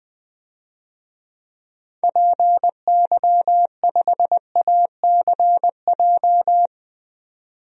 Morse code, heard as a keyed tone, about 20 wpm, 700 Hz. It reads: PY5ACJ
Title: DXCC-100 Call Signs - 6 Characters - 20wpm